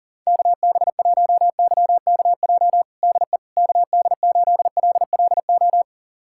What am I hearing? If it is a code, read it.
Morse code, heard as a keyed tone, about 40 wpm, 700 Hz. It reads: KB1YKJ DE KD8LLO
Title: USA Callsign DE Callsign - 6 Characters - 40wpm